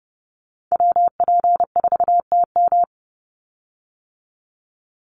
Morse code, heard as a keyed tone, about 30 wpm, 700 Hz. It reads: WP4TM
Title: USA Call Signs 30wpm